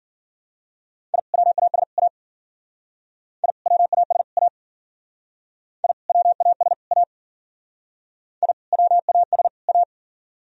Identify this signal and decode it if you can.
Morse code, heard as a keyed tone, about 53 wpm, 700 Hz. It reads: I WAS A I WAS A I WAS A I WAS A